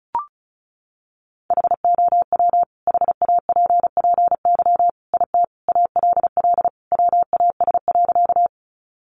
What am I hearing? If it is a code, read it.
Morse code, heard as a keyed tone, about 34 wpm, 700 Hz. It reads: HOW HAPPY IT ALL WAS.
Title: Sentences from Top 1000 Words 35wpm